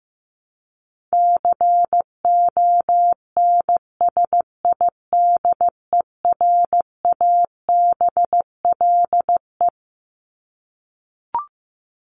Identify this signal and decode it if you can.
Morse code, heard as a keyed tone, about 15 wpm, 700 Hz. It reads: CONSIDERABLE
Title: LY Suffix - Top 100 Words with Base Words 15wpm